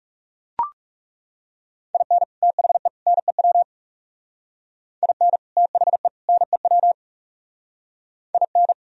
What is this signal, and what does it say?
Morse code, heard as a keyed tone, about 37 wpm, 690 Hz. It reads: IN THE NEW IN THE NEW IN
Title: Speed-Racing - Sets of 3 Words 30wpm